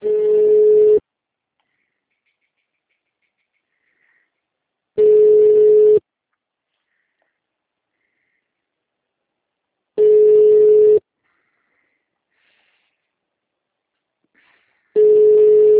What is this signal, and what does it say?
Ringback tone (Europe, 425 Hz).